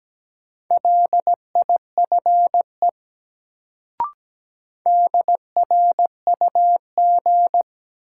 Morse code, heard as a keyed tone, about 17 wpm, 700 Hz. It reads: LIFE DRUG
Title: Sets of 5 Words - New Word Drill - 17wpm